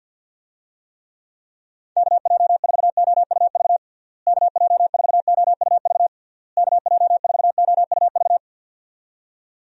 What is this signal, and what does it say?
Morse code, heard as a keyed tone, about 50 wpm, 700 Hz. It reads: KJ4OUV KJ4OUV KJ4OUV
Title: USA Call Signs - 6 Characters - Repeated 3x - 50wpm